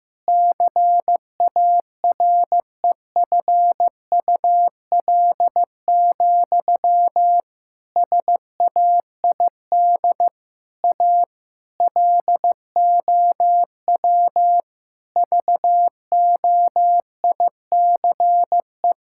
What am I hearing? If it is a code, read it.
Morse code, heard as a keyed tone, about 15 wpm, 700 Hz. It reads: CAREFUL, SAID A LOW VOICE
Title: Sentences from Top 1600 Words 15wpm